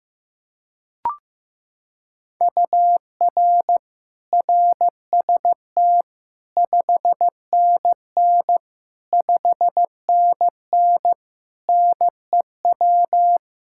Morse code, heard as a keyed tone, about 15 wpm, 700 Hz. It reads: UR RST 5NN 5NN NEW
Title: Practice QSO - Exchange 1 - Beginning - 15wpm